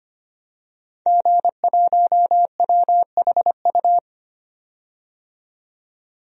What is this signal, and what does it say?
Morse code, heard as a keyed tone, about 25 wpm, 700 Hz. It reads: G1WHU
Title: DXCC-100 Call Signs - 5 Characters - 25wpm